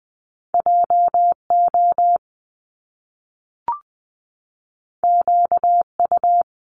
Morse code, heard as a keyed tone, about 20 wpm, 700 Hz. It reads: JO QU
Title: Top Two Letter Combinations in Words 20wpm